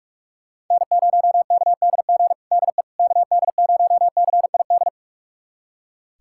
Morse code, heard as a keyed tone, about 41 wpm, 700 Hz. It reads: N0KDG DE KD0CID
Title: USA Callsign DE Callsign 45wpm